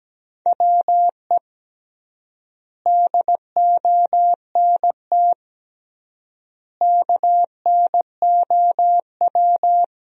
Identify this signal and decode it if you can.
Morse code, heard as a keyed tone, about 17 wpm, 700 Hz. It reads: WE DONT KNOW
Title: 3x Word-Spacing - Sentences from Top 400 Words - 17wpm